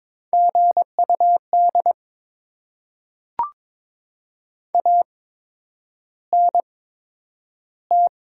Morse code, heard as a keyed tone, about 22 wpm, spaced as for 9 wpm, 700 Hz. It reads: GUD ANT
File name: Top 20 QSO Elements - 8x Character Spacing - 22wpm